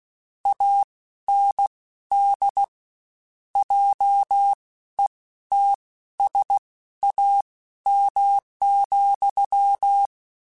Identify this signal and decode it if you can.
Morse code, heard as a keyed tone, about 16 wpm, spaced as for 12 wpm, 800 Hz. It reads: AND JETSAM,
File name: Misc / Misc-mp3s / WordPairs sp @15